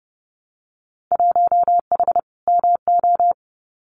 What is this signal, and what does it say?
Morse code, heard as a keyed tone, about 32 wpm, 700 Hz. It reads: 1H MO
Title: Winter Field Day Exchanges 30wpm